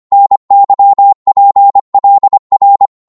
Morse code, heard as a keyed tone, about 25 wpm, 800 Hz. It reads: NYPLR
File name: Morse / ltr-25wpm